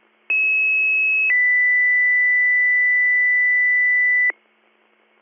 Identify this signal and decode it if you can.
Two-tone page, heard as a one-second tone, then a three-second tone, about 2600 then 2000 Hz. It is heard from 0.3 to 4.3 s.